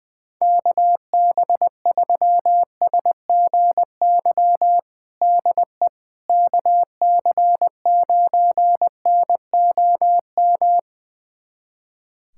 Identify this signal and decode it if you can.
Morse code, heard as a keyed tone, about 20 wpm, 700 Hz. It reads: KB3SGY DE KC9NOM
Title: USA Callsign DE Callsign - 6 Characters - 20wpm